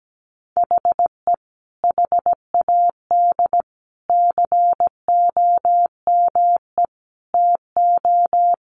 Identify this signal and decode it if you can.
Morse code, heard as a keyed tone, about 17 wpm, 700 Hz. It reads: HE HAD COME TO
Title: Sentences from Top 1500 Words 17wpm